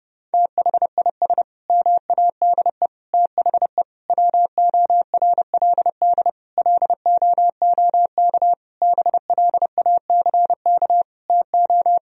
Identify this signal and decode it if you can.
Morse code, heard as a keyed tone, about 32 wpm, 700 Hz. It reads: THIS MADE THE WORLD LOOK BLACK TO
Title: Sentences from Top 1000 Words - Review Part 2 - 30wpm